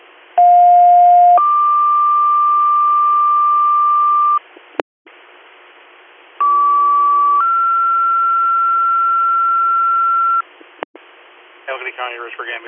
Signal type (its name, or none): two-tone page